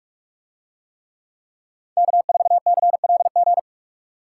Morse code, heard as a keyed tone, about 49 wpm, 700 Hz. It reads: K4CLG